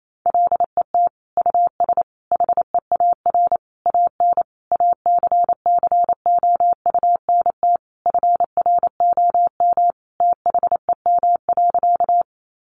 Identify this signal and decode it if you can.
Morse code, heard as a keyed tone, about 28 wpm, 700 Hz. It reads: LET US HEAR AN ACCOUNT FROM THEM.
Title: Sentences from Top 900 Words 28wpm